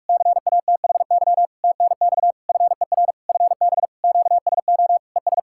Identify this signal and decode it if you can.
Morse code, heard as a keyed tone, about 46 wpm, 700 Hz. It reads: CATHY TNX FER FB QSO ES
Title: K1USN SST QSO Practice 45wpm